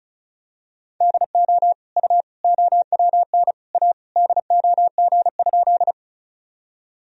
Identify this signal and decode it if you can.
Morse code, heard as a keyed tone, about 36 wpm, 700 Hz. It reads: DO U OWN A DOG?
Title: Sentences from Top 400 Words and Abbreviations 35wpm